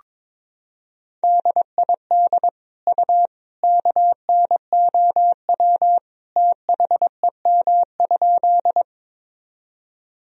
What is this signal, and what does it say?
Morse code, heard as a keyed tone, about 22 wpm, 700 Hz. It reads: DID U KNOW THEM?
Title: Sentences from Top 100 Words and Abbreviations 22wpm